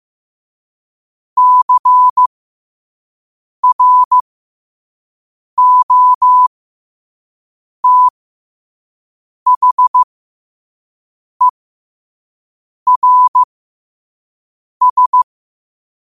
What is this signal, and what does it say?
Morse code, heard as a keyed tone, about 15 wpm, spaced as for 5 wpm, 1000 Hz. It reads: CROTHERS